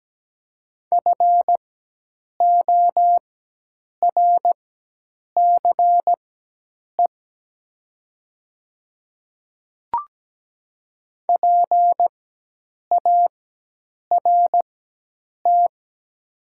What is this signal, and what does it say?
Morse code, heard as a keyed tone, about 17 wpm, spaced as for 8 wpm, 700 Hz. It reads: FORCE PART
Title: 4x Character Spacing - Top 500 Words - 17wpm